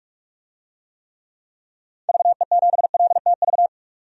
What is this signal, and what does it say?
Morse code, heard as a keyed tone, about 41 wpm, 700 Hz. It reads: VE7LTV
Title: DXCC-100 Call Signs - 6 Characters - 45wpm